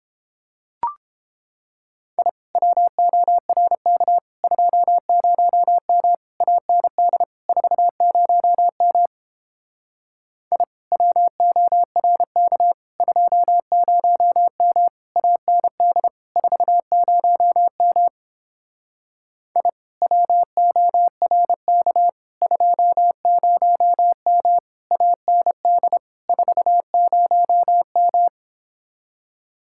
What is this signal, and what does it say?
Morse code, heard as a keyed tone, about 29 wpm, 700 Hz. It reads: I WORK 20M AND 40M I WORK 20M AND 40M I WORK 20M AND 40M